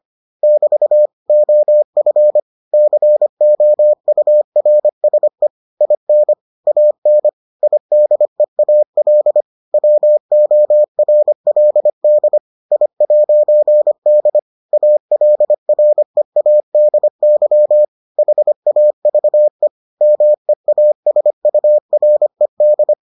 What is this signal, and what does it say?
Morse code, heard as a keyed tone, about 25 wpm, 600 Hz. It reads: <BT> OF COURSE IN AN IDEAL WORLD I'D ALREADY HAVE MEASURED